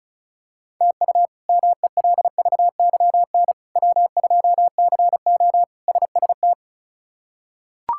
Morse code, heard as a keyed tone, about 34 wpm, 700 Hz. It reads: TU MELVYN W2CO SST
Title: K1USN SST QSO Practice - 35wpm